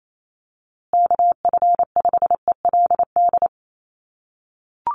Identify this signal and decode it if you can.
Morse code, heard as a keyed tone, about 28 wpm, 700 Hz. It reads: KF5ELD
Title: USA Call Signs Repeated 2x 28wpm